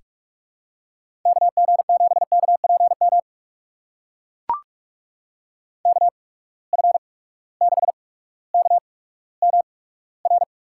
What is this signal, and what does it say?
Morse code, heard as a keyed tone, about 44 wpm, spaced as for 19 wpm, 700 Hz. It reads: KG7KPM KF6KMR
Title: USA Call Signs - 6 Characters - 8x Character Spacing - 45wpm